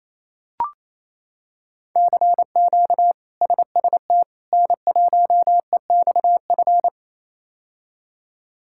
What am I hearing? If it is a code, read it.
Morse code, heard as a keyed tone, about 28 wpm, 700 Hz. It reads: CQ SST N1EXF
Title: K1USN SST QSO Practice - 28wpm